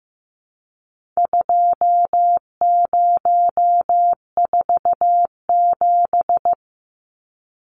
Morse code, heard as a keyed tone, about 15 wpm, 700 Hz. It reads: 2047
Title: Numbers - 4-Digit 15wpm